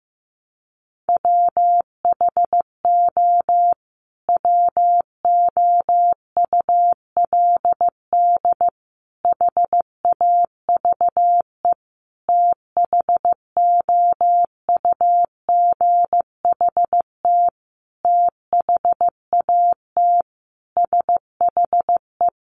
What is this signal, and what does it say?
Morse code, heard as a keyed tone, about 15 wpm, 700 Hz. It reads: WHO WOULD HAVE THOUGHT THAT SHE
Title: Sentences from Top 1200 Words 15wpm